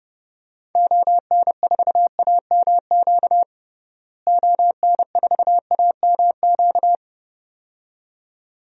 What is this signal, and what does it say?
Morse code, heard as a keyed tone, about 32 wpm, 700 Hz. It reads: ON4AMQ ON4AMQ
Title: DXCC-100 Call Signs - 6 Characters - Repeated 2x - 30wpm